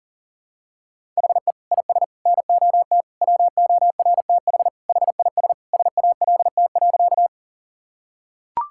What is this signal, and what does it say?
Morse code, heard as a keyed tone, about 40 wpm, 700 Hz. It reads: HE IS NOT WORTH HIS SALT.
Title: Sentences from Top 1000 Words - Review Part 5 - 40wpm